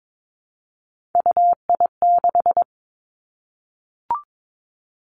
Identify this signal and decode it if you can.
Morse code, heard as a keyed tone, about 22 wpm, 700 Hz. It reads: UI6
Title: Three Challenging Letter-Number Combinations 22wpm